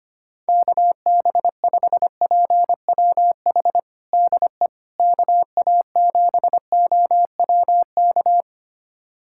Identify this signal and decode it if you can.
Morse code, heard as a keyed tone, about 25 wpm, 700 Hz. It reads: KB5PWH DE KA7OWK
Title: USA Callsign DE Callsign 25wpm